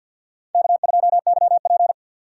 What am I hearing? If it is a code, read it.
Morse code, heard as a keyed tone, about 52 wpm, 700 Hz. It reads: K2YP
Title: USA Call Signs - 4 Characters - 2x Character Spacing - 50wpm